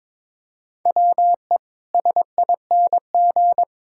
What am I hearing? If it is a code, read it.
Morse code, heard as a keyed tone, about 22 wpm, 700 Hz. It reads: WE SING
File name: practice / 22wpm / Easy Sentences with Hints - 22wpm